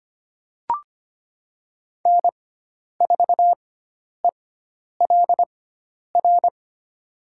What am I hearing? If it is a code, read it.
Morse code, heard as a keyed tone, about 25 wpm, spaced as for 10 wpm, 700 Hz. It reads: N4ELR
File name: Lesson 16g - Letter c - Call Signs 25wpm